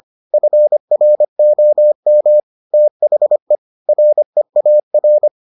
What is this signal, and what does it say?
Morse code, heard as a keyed tone, about 25 wpm, 600 Hz. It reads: FROM THE REAR